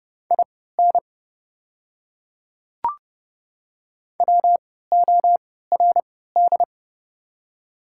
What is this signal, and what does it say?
Morse code, heard as a keyed tone, about 30 wpm, spaced as for 17 wpm, 700 Hz. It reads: IN WORD
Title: 3x Character Spacing - Top 100 Words - 30wpm